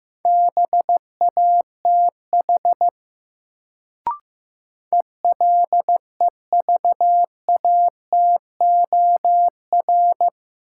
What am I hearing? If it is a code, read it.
Morse code, heard as a keyed tone, about 15 wpm, 700 Hz. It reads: BATH ELEVATOR
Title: Sentences from Top 1600 Words - New Word Drill Alt - 15wpm